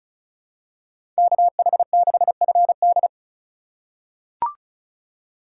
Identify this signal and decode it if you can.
Morse code, heard as a keyed tone, about 36 wpm, 700 Hz. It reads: KH6FD
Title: USA Call Signs - 5 Characters - 4x Character Spacing - 35wpm